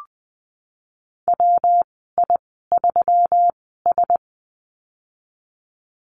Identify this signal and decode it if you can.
Morse code, heard as a keyed tone, about 20 wpm, spaced as for 15 wpm, 700 Hz. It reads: WI3S
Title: 147 Lesson 24g - Letter y - Call Signs 20wpm